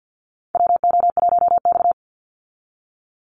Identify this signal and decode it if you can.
Morse code, heard as a keyed tone, about 50 wpm, 700 Hz. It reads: WO1X
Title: Call Signs Repeated 2x - 50wpm